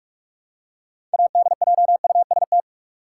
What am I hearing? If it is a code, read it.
Morse code, heard as a keyed tone, about 46 wpm, 700 Hz. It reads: ADJUST
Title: Six Letter Words - 45wpm